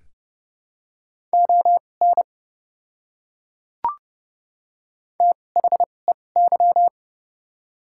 Morse code, heard as a keyed tone, about 30 wpm, spaced as for 22 wpm, 700 Hz. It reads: ON THEY
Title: 2x Character Spacing - Top 200 Words - 30wpm